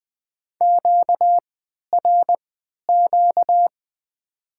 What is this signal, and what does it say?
Morse code, heard as a keyed tone, about 20 wpm, spaced as for 11 wpm, 700 Hz. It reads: QRQ
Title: Top 90 QSO Elements - 3x Character Spacing - 20wpm